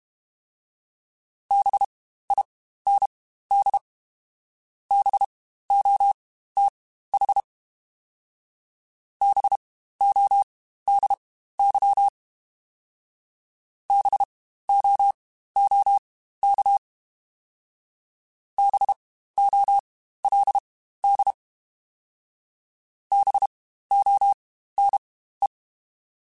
Morse code, heard as a keyed tone, about 32 wpm, spaced as for 15 wpm, 800 Hz. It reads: BIND BOTH BODY BOOK BOLD BONE
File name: Spaced Alphas / Spaced Alphas mp3s / Bs sp @30